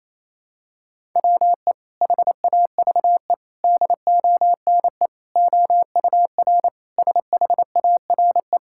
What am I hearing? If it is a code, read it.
Morse code, heard as a keyed tone, about 28 wpm, 700 Hz. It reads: WE HAVE DONE OUR SHARE
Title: Sentences from Top 1000 Words - Review Part 6 - 28wpm